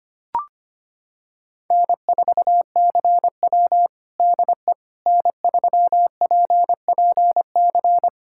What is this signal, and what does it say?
Morse code, heard as a keyed tone, about 25 wpm, 700 Hz. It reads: N4CW DE N3PPC
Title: Practice QSO - Exchange 4 - Ending - 25wpm